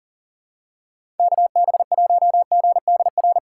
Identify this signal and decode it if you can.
Morse code, heard as a keyed tone, about 40 wpm, 700 Hz. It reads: KB1GDR
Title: USA Call Signs - 6 Characters - 6x Character Spacing - 40wpm